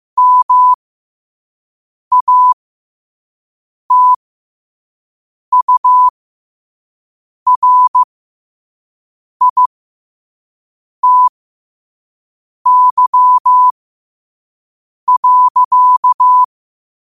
Morse code, heard as a keyed tone, about 15 wpm, spaced as for 5 wpm, 1000 Hz. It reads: MATURITY.
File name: qotdfiles / 5wpm